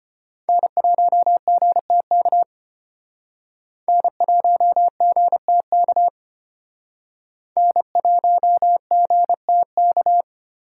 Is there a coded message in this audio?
Yes: Morse code from 0.5 to 10.2 s, heard as a keyed tone, about 32 wpm, 700 Hz. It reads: N1GTK N1GTK N1GTK